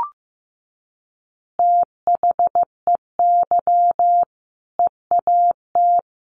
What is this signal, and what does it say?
Morse code, heard as a keyed tone, about 15 wpm, 700 Hz. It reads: THEY EAT
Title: Easy Phrases 15wpm